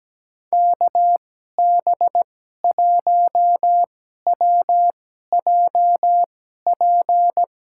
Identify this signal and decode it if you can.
Morse code, heard as a keyed tone, about 17 wpm, spaced as for 12 wpm, 700 Hz. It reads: KB1WJP
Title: USA Call Signs - 2x Character Spacing - 17wpm